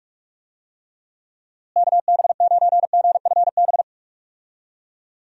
Morse code, heard as a keyed tone, about 41 wpm, 700 Hz. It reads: KB9GFB
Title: USA Call Signs - 6 Characters - Repeated 2x - 45wpm